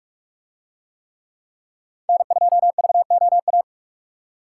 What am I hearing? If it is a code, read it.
Morse code, heard as a keyed tone, about 42 wpm, 700 Hz. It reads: N2VOA